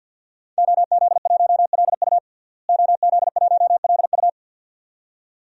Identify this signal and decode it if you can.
Morse code, heard as a keyed tone, about 50 wpm, 700 Hz. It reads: OZ1LU OZ1LU
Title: DXCC-100 Call Signs - 5 Characters - Repeated 2x - 50wpm